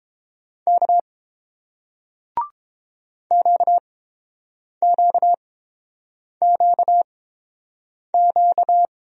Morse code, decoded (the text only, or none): K QQQQ